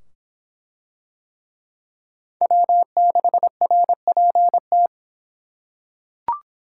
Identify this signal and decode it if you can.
Morse code, heard as a keyed tone, about 25 wpm, 700 Hz. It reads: W6RPT